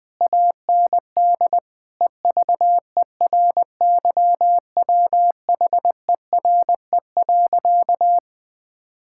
Morse code, heard as a keyed tone, about 21 wpm, 700 Hz. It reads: AND EVERYWHERE.
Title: Sentences from Top 1600 Words 20wpm